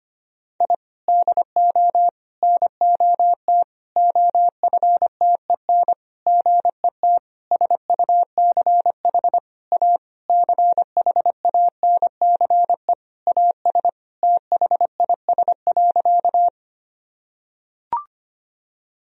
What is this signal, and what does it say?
Morse code, heard as a keyed tone, about 25 wpm, 700 Hz. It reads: I DO NOT OFTEN GET SUCH A CHANCE AS THIS.
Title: Sentences from Top 1000 Words - Review Part 1 - 25wpm